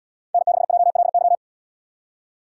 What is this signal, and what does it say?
Morse code, heard as a keyed tone, about 78 wpm, 700 Hz. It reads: N6JLY